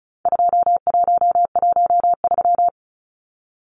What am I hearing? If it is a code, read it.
Morse code, heard as a keyed tone, about 34 wpm, 700 Hz. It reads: 2113
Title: Numbers - 4-Digit 35wpm